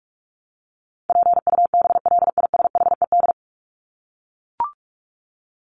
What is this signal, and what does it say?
Morse code, heard as a keyed tone, about 44 wpm, 700 Hz. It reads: PUBLISHED
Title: ED Suffix - Top 100 Words with Base Words 45wpm